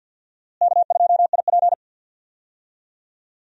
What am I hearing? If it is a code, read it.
Morse code, heard as a keyed tone, about 51 wpm, 700 Hz. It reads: K2IP
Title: USA Call Signs - 4 Characters - Repeated 3x - 50wpm